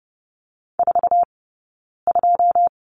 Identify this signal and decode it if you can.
Morse code, heard as a keyed tone, about 30 wpm, spaced as for 9 wpm, 690 Hz. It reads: 42
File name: Lesson 17d - Letter u - Two Characters 30wpm